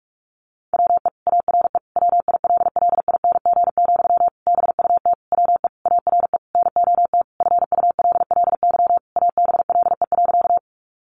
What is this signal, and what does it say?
Morse code, heard as a keyed tone, about 44 wpm, 700 Hz. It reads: WE ARE WILLING, BUT WE ARE NOT FULLY ABLE.
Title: Sentences from Top 1300 Words 45wpm